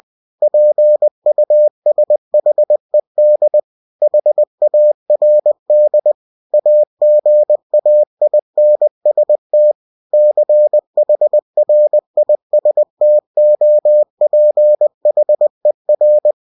Morse code, heard as a keyed tone, about 20 wpm, 600 Hz. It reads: PUSHED HARD AGAINST CHRISTOPHER
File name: cw-books / winnie-the-pooh / winnie-20wpm-page-0076